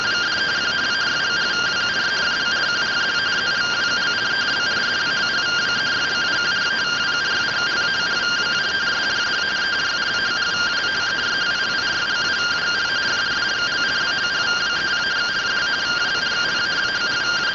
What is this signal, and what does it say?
Two-tone teleprinter signal, about 170 Hz shift.